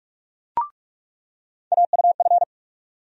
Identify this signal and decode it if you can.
Morse code, heard as a keyed tone, about 43 wpm, 700 Hz. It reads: AUF
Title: USA Call Sign Postfixes - 3 Letters 45wpm